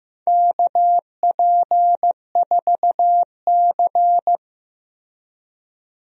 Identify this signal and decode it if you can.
Morse code, heard as a keyed tone, about 15 wpm, 700 Hz. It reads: KP4C
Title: DXCC-100 Call Signs - 4 Characters - Repeated 3x - 15wpm